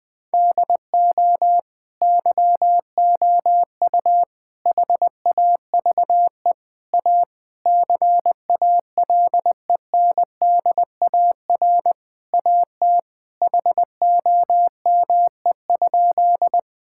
Morse code, heard as a keyed tone, about 21 wpm, 700 Hz. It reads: DO YOU HAVE A CALENDAR AT HOME?
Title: Sentences from Top 1600 Words 20wpm